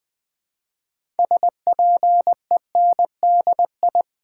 Morse code, heard as a keyed tone, about 20 wpm, 700 Hz. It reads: SPENDI
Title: Build A Word - 3 Letters At A Time - 8-Letter Words - 20wpm